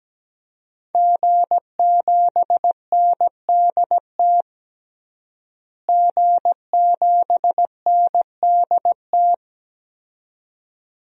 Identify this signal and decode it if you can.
Morse code, heard as a keyed tone, about 17 wpm, 700 Hz. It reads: G7NDT G7NDT
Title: DXCC-100 Call Signs - 5 Characters - Repeated 2x - 17wpm